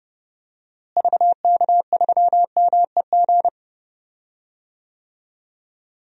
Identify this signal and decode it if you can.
Morse code, heard as a keyed tone, about 30 wpm, 700 Hz. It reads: VK3MEG
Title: DXCC-100 Call Signs - 6 Characters - Repeated 3x - 30wpm